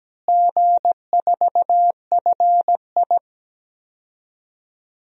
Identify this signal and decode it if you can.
Morse code, heard as a keyed tone, about 17 wpm, 700 Hz. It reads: G4FI
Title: DXCC-100 Call Signs - 4 Characters - Repeated 3x - 17wpm